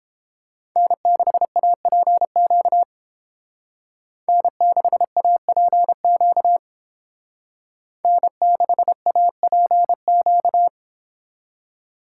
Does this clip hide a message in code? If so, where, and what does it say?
Yes: Morse code from 0.8 to 10.7 s, heard as a keyed tone, about 29 wpm, 700 Hz. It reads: N6APQ N6APQ N6APQ